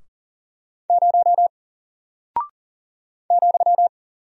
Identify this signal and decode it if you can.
Morse code, heard as a keyed tone, about 39 wpm, 700 Hz. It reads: 0 ,
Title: Single Character (Letter:Number:Punctuation:Prosign) Alt 40wpm